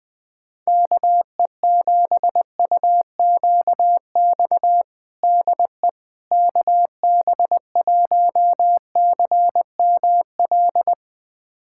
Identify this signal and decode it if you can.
Morse code, heard as a keyed tone, about 20 wpm, 700 Hz. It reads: KE7UQX DE KB1CML
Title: USA Callsign DE Callsign 20wpm